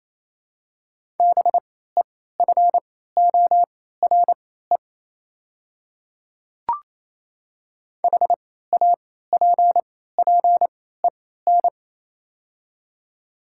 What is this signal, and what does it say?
Morse code, heard as a keyed tone, about 27 wpm, spaced as for 16 wpm, 700 Hz. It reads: BEFORE HAPPEN
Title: 3x Character Spacing - Top 500 Words - 28wpm